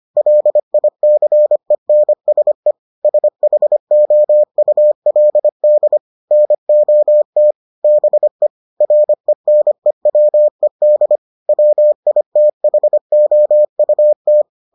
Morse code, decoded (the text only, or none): LICENSE SHOULD NOT BE RENEWED WITHOUT